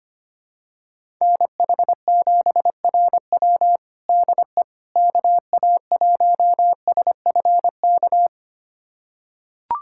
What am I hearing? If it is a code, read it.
Morse code, heard as a keyed tone, about 25 wpm, 700 Hz. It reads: NH7RW DE KA1SFK
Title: USA Callsign DE Callsign 25wpm